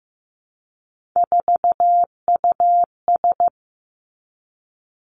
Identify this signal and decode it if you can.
Morse code, heard as a keyed tone, about 15 wpm, 700 Hz. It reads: 4US
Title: Three Challenging Letter-Number Combinations 15wpm